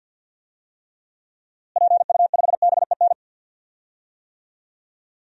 Morse code, heard as a keyed tone, about 51 wpm, 700 Hz. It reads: PU5BEN
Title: DXCC-100 Call Signs - 6 Characters - Repeated 3x - 50wpm